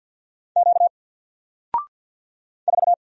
Morse code, heard as a keyed tone, about 50 wpm, 700 Hz. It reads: Q 4
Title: Single Letter-Number 50wpm